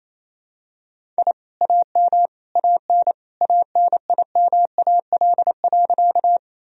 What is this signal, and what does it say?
Morse code, heard as a keyed tone, about 28 wpm, 700 Hz. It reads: I AM AN ANIMAL.
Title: Sentences from Top 1000 Words - Review Part 1 - 28wpm